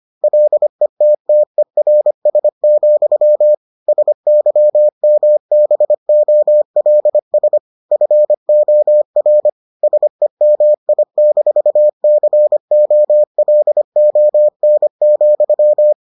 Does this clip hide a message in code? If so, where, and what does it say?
Yes: Morse code from 0.2 to 15.9 s, heard as a keyed tone, about 25 wpm, 600 Hz. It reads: LETTERS, SYMBOLS FOR SEMI-COLON,